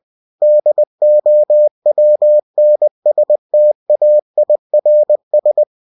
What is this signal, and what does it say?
Morse code, heard as a keyed tone, about 20 wpm, 600 Hz. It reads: DOWNSTAIRS
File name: cw-books / winnie-the-pooh / winnie-20wpm-page-0000